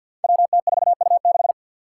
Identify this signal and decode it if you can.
Morse code, heard as a keyed tone, about 52 wpm, 700 Hz. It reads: WT4U6
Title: Five Challenging Letter-Number Combinations 50wpm